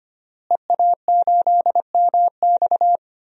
Morse code, heard as a keyed tone, about 25 wpm, 700 Hz. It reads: EA8MX
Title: DXCC-100 Call Signs - 5 Characters - Repeated 3x - 25wpm